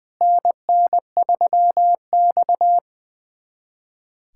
Morse code, heard as a keyed tone, about 21 wpm, 700 Hz. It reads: NN3X
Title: USA Callsign DE Callsign - 4 Characters - 20wpm